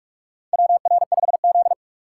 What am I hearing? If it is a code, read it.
Morse code, heard as a keyed tone, about 45 wpm, 700 Hz. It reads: WR5Z
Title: USA Call Signs - 3x Character Spacing - 45wpm